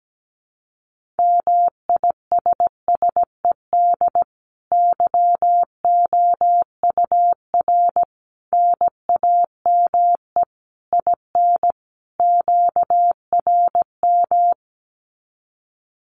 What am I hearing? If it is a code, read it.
Morse code, heard as a keyed tone, about 17 wpm, 700 Hz. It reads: MISSED YOUR NAME IN QRM
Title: Common QSO Phrases 17wpm